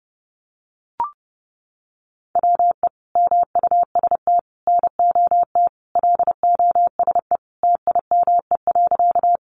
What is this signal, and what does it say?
Morse code, heard as a keyed tone, about 30 wpm, 700 Hz. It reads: WE MUST NOT LOSE TIME.
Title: Sentences from Top 1100 Words 30wpm